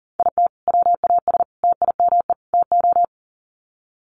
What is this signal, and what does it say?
Morse code, heard as a keyed tone, about 40 wpm, 700 Hz. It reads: IT WAS TIME TO
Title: Sets of 4 Words 40wpm